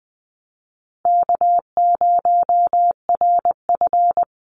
Morse code, heard as a keyed tone, about 20 wpm, 700 Hz. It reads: K0RF
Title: Easy Call Signs Repeated 3x - 20wpm